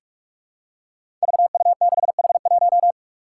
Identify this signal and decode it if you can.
Morse code, heard as a keyed tone, about 46 wpm, 700 Hz. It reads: VU6H1
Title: Five Challenging Letter-Number Combinations 45wpm